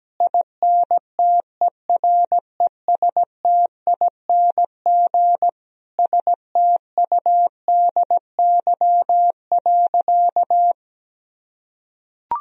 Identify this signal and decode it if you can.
Morse code, heard as a keyed tone, about 17 wpm, 700 Hz. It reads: INTERESTING STUDY.
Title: Sentences from Top 1000 Words - Review Part 1 - 17wpm